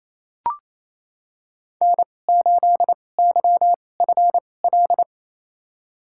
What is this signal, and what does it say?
Morse code, heard as a keyed tone, about 28 wpm, spaced as for 20 wpm, 700 Hz. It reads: N8YFL
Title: USA Call Signs - 5 Characters - 2x Character Spacing - 28wpm